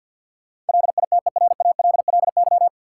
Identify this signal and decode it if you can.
Morse code, heard as a keyed tone, about 47 wpm, 700 Hz. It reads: LITERALLY
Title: Sentences from Top 1400 Words - New Word Drill - 50wpm